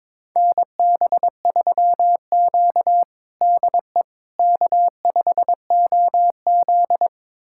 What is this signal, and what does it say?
Morse code, heard as a keyed tone, about 22 wpm, 700 Hz. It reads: NB3Q DE K5OZ
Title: USA Callsign DE Callsign - 4 Characters - 22wpm